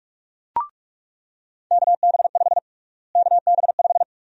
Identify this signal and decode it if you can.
Morse code, heard as a keyed tone, about 46 wpm, 700 Hz. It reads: KB5 KB5
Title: USA Call Sign Prefixes - Repeated 2x - 45wpm